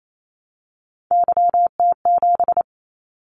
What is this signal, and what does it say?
Morse code, heard as a keyed tone, about 28 wpm, 700 Hz. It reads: YT7
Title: Callsign Prefixes for Top 25 DXCC - 28wpm